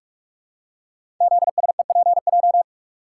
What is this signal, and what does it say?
Morse code, heard as a keyed tone, about 44 wpm, 700 Hz. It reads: ZSEPJ
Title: Code Groups - 5 Letters - 45wpm